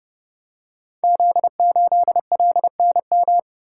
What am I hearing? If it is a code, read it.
Morse code, heard as a keyed tone, about 28 wpm, 700 Hz. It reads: Z8LNM
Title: Code Groups - 5 Letter:Numbers - 30wpm